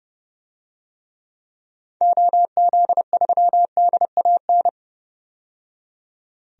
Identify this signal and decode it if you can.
Morse code, heard as a keyed tone, about 31 wpm, 700 Hz. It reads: OZ3DAN
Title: DXCC-100 Call Signs - 6 Characters - 30wpm